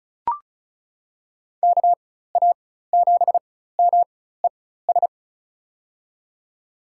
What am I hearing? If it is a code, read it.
Morse code, heard as a keyed tone, about 34 wpm, spaced as for 16 wpm, 700 Hz. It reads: KA7MES
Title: USA Call Signs - 6 Characters - 4x Character Spacing - 35wpm